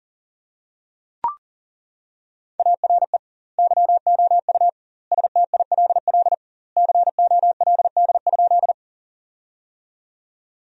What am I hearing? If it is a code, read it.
Morse code, heard as a keyed tone, about 40 wpm, 700 Hz. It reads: ARE YOU STILL COLD?
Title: 2x Word-Spacing - Sentences from Top 400 Words - 40wpm